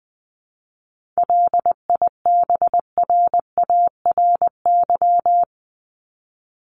Morse code, heard as a keyed tone, about 20 wpm, 700 Hz. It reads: LIBRARY
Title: Top 501-750 Words 20wpm